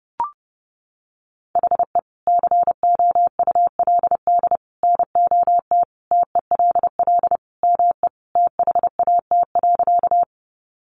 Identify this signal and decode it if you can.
Morse code, heard as a keyed tone, about 30 wpm, 700 Hz. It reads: HE COULD NOT TELL ME THAT.
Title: Sentences from Top 600 Words 30wpm